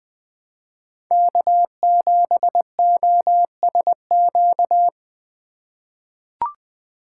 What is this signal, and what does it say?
Morse code, heard as a keyed tone, about 20 wpm, 700 Hz. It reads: K7OSQ
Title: USA Call Signs - 5 Characters - 4x Character Spacing - 20wpm